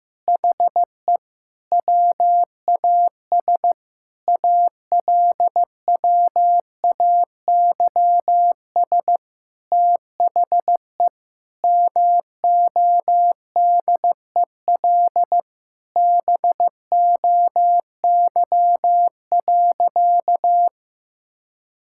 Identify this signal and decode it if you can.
Morse code, heard as a keyed tone, about 15 wpm, 700 Hz. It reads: HE WAS ALWAYS THE MODEL BOY.
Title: Sentences from Top 1000 Words - Review Part 5 - 15wpm